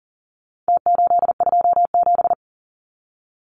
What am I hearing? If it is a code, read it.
Morse code, heard as a keyed tone, about 40 wpm, 700 Hz. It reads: T827
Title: Numbers - Contest Style - 4-Digit 40wpm